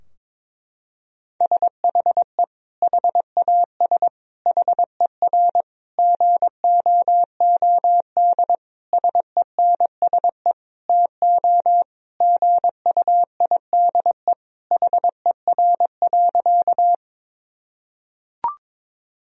Morse code, heard as a keyed tone, about 22 wpm, 700 Hz. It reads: SHE HAS HER GOOD SENSE TO GUIDE HER.
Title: Sentences from Top 1000 Words - Review Part 4 - 22wpm